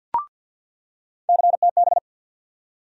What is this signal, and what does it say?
Morse code, heard as a keyed tone, about 48 wpm, 700 Hz. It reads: CTB
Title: USA Call Sign Postfixes - 3 Letters 50wpm